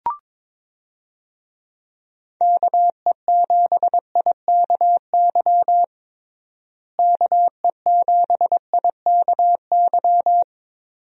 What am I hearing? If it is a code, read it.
Morse code, heard as a keyed tone, about 22 wpm, 700 Hz. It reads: KE7IKY KE7IKY
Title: USA Call Signs - 6 Characters - Repeated 2x - 22wpm